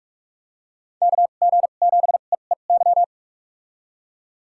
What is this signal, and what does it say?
Morse code, heard as a keyed tone, about 46 wpm, spaced as for 33 wpm, 700 Hz. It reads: KG7EEY
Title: USA Call Signs - 2x Character Spacing - 45wpm